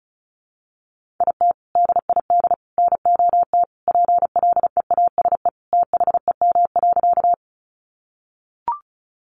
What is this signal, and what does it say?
Morse code, heard as a keyed tone, about 34 wpm, 700 Hz. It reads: IT DID NOT PLEASE THEM.
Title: Sentences from Top 1500 Words 35wpm